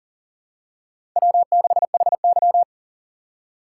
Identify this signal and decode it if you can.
Morse code, heard as a keyed tone, about 40 wpm, 700 Hz. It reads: W6HY
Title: USA Call Signs - 4 Characters - 3x Character Spacing - 40wpm